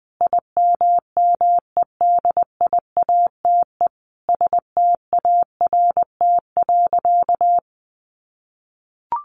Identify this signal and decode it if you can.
Morse code, heard as a keyed tone, about 20 wpm, 700 Hz. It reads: IMMEDIATE START.
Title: Sentences from Top 1200 Words 20wpm